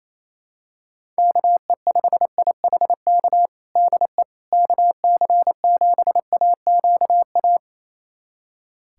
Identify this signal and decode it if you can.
Morse code, heard as a keyed tone, about 28 wpm, 700 Hz. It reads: KE5IHK DE KC7AQA
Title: USA Callsign DE Callsign - 6 Characters - 28wpm